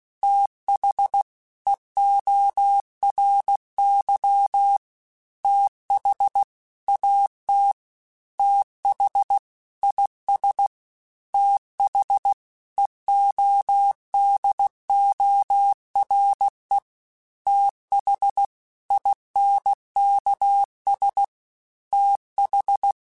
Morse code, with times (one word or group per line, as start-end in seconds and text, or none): 0.2-1.2 TH
1.7-4.8 EORY
5.4-6.4 TH
6.9-7.7 AT
8.4-9.4 TH
9.8-10.7 IS
11.3-12.3 TH
12.8-16.8 EODORE
17.5-18.5 TH
18.9-21.2 INKS
21.9-22.9 TH